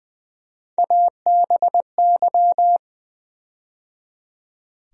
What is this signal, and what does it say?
Morse code, heard as a keyed tone, about 20 wpm, 700 Hz. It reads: ABY
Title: USA Call Sign Postfixes - 3 Letters 20wpm